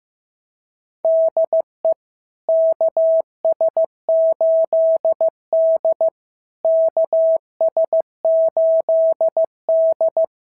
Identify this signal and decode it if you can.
Morse code, heard as a keyed tone, about 15 wpm, 650 Hz. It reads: DE KS8D KS8D
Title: SOTA QSO Practice 15wpm